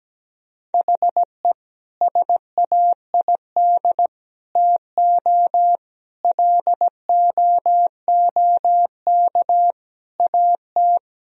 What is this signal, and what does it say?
Morse code, heard as a keyed tone, about 17 wpm, 700 Hz. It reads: HE SAID TO LOOK AT
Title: Sentences from Top 1000 Words - Review Part 1 - 17wpm